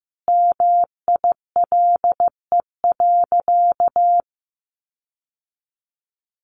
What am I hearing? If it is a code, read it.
Morse code, heard as a keyed tone, about 15 wpm, 700 Hz. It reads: MILE.
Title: Sentences from Top 1300 Words 15wpm